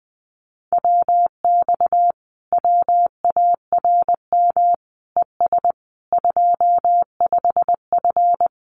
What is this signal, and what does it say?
Morse code, heard as a keyed tone, about 20 wpm, 700 Hz. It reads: WX WARM ES 25F
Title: Common QSO Phrases Encore 20wpm